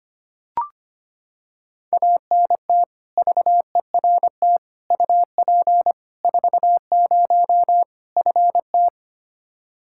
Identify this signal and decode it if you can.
Morse code, heard as a keyed tone, about 25 wpm, 700 Hz. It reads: ANT VERT UP 40 FT
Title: Practice QSO - Exchange 2 - Info - 25wpm